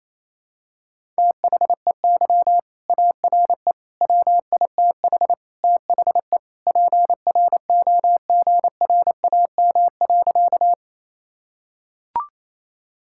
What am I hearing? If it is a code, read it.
Morse code, heard as a keyed tone, about 28 wpm, 700 Hz. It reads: THEY ARE WITH THE PROGRAM.
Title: Sentences from Top 1000 Words - Review Part 5 - 28wpm